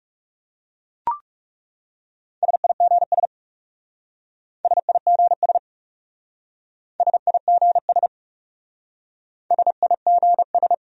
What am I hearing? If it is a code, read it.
Morse code, heard as a keyed tone, about 38 wpm, 700 Hz. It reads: SIGS SIGS SIGS SIGS